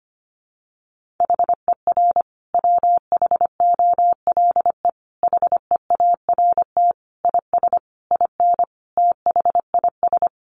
Morse code, heard as a keyed tone, about 25 wpm, 700 Hz. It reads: HER WHOLE HEART IS IN THIS
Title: Sentences from Top 1500 Words 25wpm